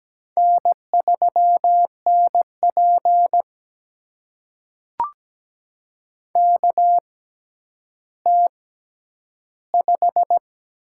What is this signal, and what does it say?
Morse code, heard as a keyed tone, about 17 wpm, spaced as for 10 wpm, 700 Hz. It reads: N3NP KT5
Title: USA Call Signs - 4 Characters - 6x Character Spacing - 17wpm